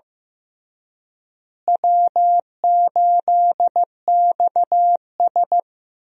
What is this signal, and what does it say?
Morse code, heard as a keyed tone, about 15 wpm, 700 Hz. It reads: W8XS
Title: USA Call Signs - 4 Characters - Repeated 3x - 15wpm